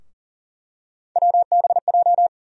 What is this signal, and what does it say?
Morse code, heard as a keyed tone, about 39 wpm, 700 Hz. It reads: WBJ
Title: USA Call Sign Postfixes - 3 Letters 40wpm